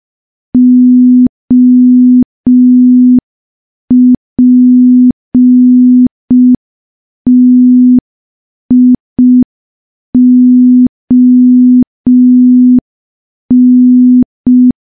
Morse code, heard as a keyed tone, about 5 wpm, 250 Hz. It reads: OPTION
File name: morse / morse-training-05WPM